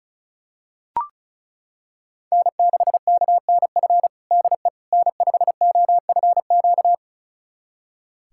Morse code, heard as a keyed tone, about 34 wpm, 700 Hz. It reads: N6KNF DE N5OFQ
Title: USA Callsign DE Callsign - 5 Characters - 35wpm